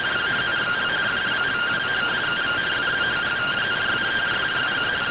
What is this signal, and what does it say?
Two-tone teleprinter signal, about 170 Hz shift.
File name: findings / CORONAL fr69 nb202 3mm.gen-37 rtty rx